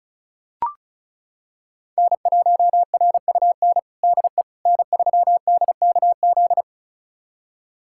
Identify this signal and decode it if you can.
Morse code, heard as a keyed tone, about 36 wpm, 690 Hz. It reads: N1RUN DE N3DKZ
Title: USA Callsign DE Callsign - 5 Characters - 35wpm